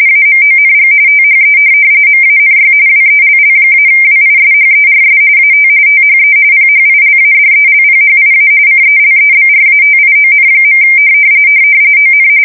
RTTY, about 45.45 baud. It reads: AMONG THE MANY WELL-WISHERS AT THE PHILADELPHIA CELEBRATION. ON
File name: fsk170 / USS-New-Jersey